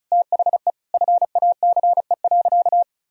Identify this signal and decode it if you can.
Morse code, heard as a keyed tone, about 36 wpm, 700 Hz. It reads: THE FACE.
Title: Sentences from Top 1000 Words - Review Part 2 - 35wpm